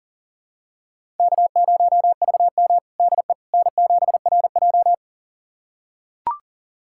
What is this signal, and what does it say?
Morse code, heard as a keyed tone, about 37 wpm, 700 Hz. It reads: K0VM DE N7RJ
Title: USA Callsign DE Callsign - 4 Characters - 40wpm